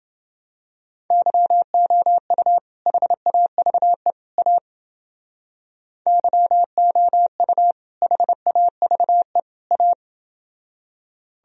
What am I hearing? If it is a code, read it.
Morse code, heard as a keyed tone, about 29 wpm, 700 Hz. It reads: YOU HAVE A YOU HAVE A